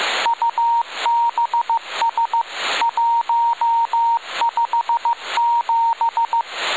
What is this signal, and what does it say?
Morse code, heard as a keyed tone, about 15 wpm, 930 Hz, with radio noise. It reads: UBS 157
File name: rs28 1900utc von dk3wn